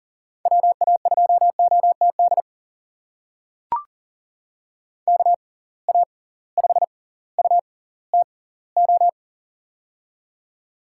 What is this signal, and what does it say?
Morse code, heard as a keyed tone, about 38 wpm, spaced as for 20 wpm, 700 Hz. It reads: WA2OTD KA5UTO
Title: USA Call Signs - 6 Characters - 6x Character Spacing - 40wpm